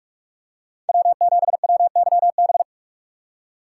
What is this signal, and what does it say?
Morse code, heard as a keyed tone, about 42 wpm, 700 Hz. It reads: W7WYB
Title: USA Call Signs - 5 Characters - 2x Character Spacing - 45wpm